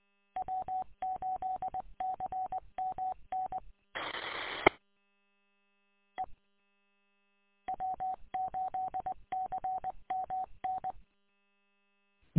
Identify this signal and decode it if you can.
Morse code, heard as a keyed tone, about 20 wpm, 750 Hz. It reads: W8CMN E E W8CMN
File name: w8cmn- -combined